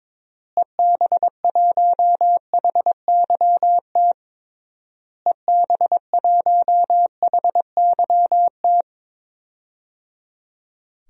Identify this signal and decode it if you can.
Morse code, heard as a keyed tone, about 21 wpm, 700 Hz. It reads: EB1HYT EB1HYT
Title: DXCC-100 Call Signs - 6 Characters - Repeated 3x - 22wpm